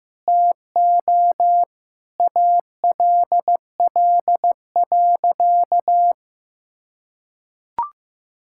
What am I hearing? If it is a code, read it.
Morse code, heard as a keyed tone, about 15 wpm, 700 Hz. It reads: TO ALL.
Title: Sentences from Top 1000 Words - Review Part 5 - 15wpm